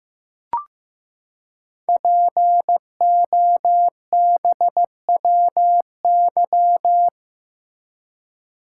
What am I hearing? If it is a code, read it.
Morse code, heard as a keyed tone, about 15 wpm, 700 Hz. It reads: POBWY
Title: Code Groups - 5 Letters - 15wpm